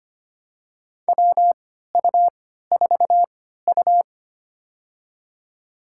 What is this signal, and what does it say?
Morse code, heard as a keyed tone, about 25 wpm, spaced as for 14 wpm, 700 Hz. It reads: WU4U
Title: USA Call Signs - 3x Character Spacing - 25wpm